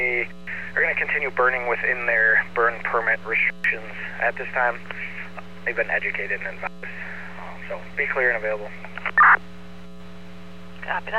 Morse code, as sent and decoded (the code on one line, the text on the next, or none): none